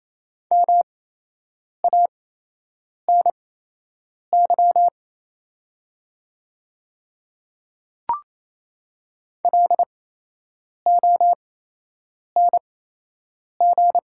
Morse code, heard as a keyed tone, about 28 wpm, spaced as for 8 wpm, 700 Hz. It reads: MANY LONG
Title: 8x Character Spacing - Top 100 Words - 28wpm